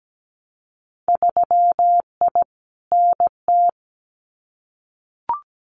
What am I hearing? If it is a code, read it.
Morse code, heard as a keyed tone, about 17 wpm, 700 Hz. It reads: 3I NT
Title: Winter Field Day Exchanges Encore 17wpm